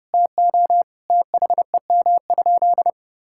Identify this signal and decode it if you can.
Morse code, heard as a keyed tone, about 30 wpm, 700 Hz. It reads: TO THEM?
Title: Sentences from Top 1000 Words - Review Part 5 - 30wpm